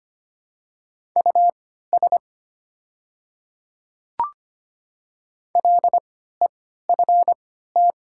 Morse code, heard as a keyed tone, about 25 wpm, spaced as for 14 wpm, 690 Hz. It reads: US LEFT
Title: 3x Character Spacing - Top 500 Words - 25wpm